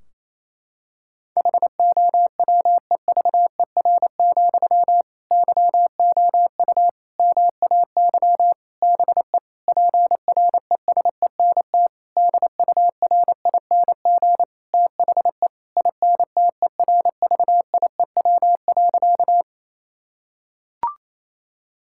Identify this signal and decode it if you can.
Morse code, heard as a keyed tone, about 28 wpm, 700 Hz. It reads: HOWEVER, YOU MAY BE PRESENT DURING THE INTERVIEW.
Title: Sentences from Top 1600 Words 28wpm